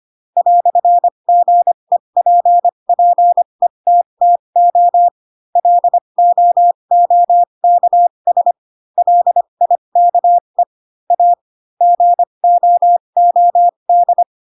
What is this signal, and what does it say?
Morse code, heard as a keyed tone, about 25 wpm, spaced as for 22 wpm, 700 Hz. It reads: "GEPPETTO LOOKS LIKE A GOOD
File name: Pinocchio / Pinocchio ch3 18 25 ik1hge